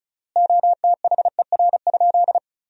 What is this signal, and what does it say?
Morse code, heard as a keyed tone, about 36 wpm, 700 Hz. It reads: OTHER?
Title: 3x Word-Spacing - Sentences from Top 200 Words - 35wpm